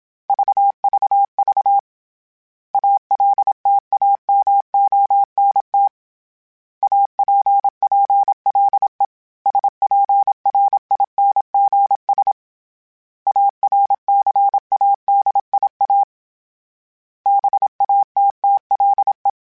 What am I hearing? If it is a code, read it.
Morse code, heard as a keyed tone, about 27 wpm, 800 Hz. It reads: VVV ALTAMONT APPLE SPRINGS ARCADIA BATTLE
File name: fromSMs / FromSMs mp3s / Cities@25